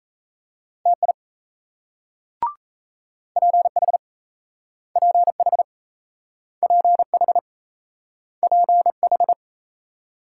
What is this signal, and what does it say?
Morse code, heard as a keyed tone, about 36 wpm, 700 Hz. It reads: TI PH PH PH PH